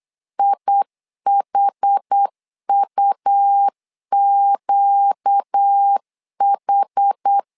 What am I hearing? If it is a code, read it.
Morse code, heard as a keyed tone, about 8 wpm, 800 Hz. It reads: IHUQH
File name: LEZIONI / 5 LBVC / 5 L 8wpm 2831